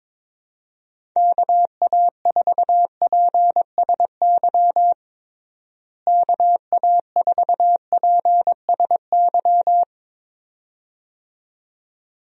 Morse code, heard as a keyed tone, about 22 wpm, 700 Hz. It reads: KA4PSY KA4PSY
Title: USA Call Signs - 6 Characters - Repeated 2x - 22wpm